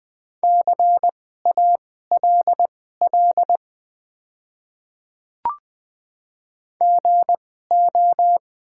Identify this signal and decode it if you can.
Morse code, heard as a keyed tone, about 20 wpm, spaced as for 14 wpm, 700 Hz. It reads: CALL GO
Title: 2x Character Spacing - Top 100 Words - 20wpm